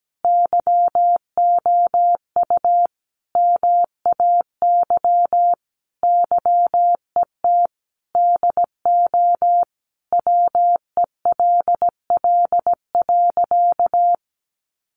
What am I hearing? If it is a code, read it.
Morse code, heard as a keyed tone, about 17 wpm, 700 Hz. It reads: YOU MAY YET DO WELL.
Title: Sentences from Top 1400 Words 17wpm